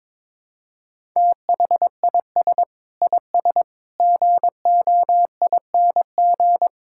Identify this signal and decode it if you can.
Morse code, heard as a keyed tone, about 21 wpm, 700 Hz. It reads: THIS IS GOING
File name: practice / 22wpm / Build A Phrase - 5-Word Phrases - 22wpm